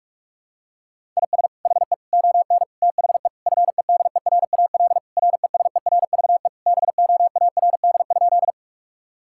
Morse code, heard as a keyed tone, about 47 wpm, 700 Hz. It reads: IS HE ON THE FEDERAL RESERVE BOARD?
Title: Sentences from Top 1000 Words - Review Part 6 - 45wpm